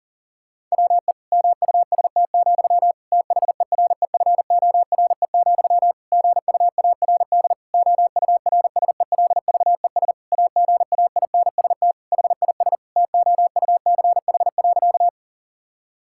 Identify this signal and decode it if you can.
Morse code, heard as a keyed tone, about 40 wpm, 700 Hz. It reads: WE MUST, THEREFORE, GUARD OURSELVES AGAINST HIS TOUCH.
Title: Sentences from Top 1600 Words 40wpm